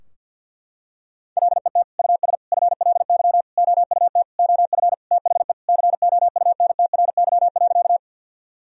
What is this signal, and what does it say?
Morse code, heard as a keyed tone, about 51 wpm, 700 Hz. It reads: LET US FLY OUT OF THE COUNTRY.
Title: Sentences from Top 1000 Words - Review Part 2 - 50wpm